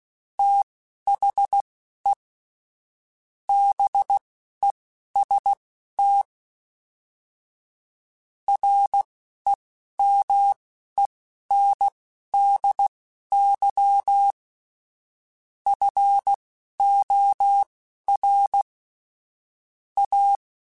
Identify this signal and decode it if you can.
Morse code, heard as a keyed tone, about 16 wpm, spaced as for 12 wpm, 800 Hz. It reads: THE BEST REMENDY FOR A
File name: Quotes / Quotes-mp3s / Temper @10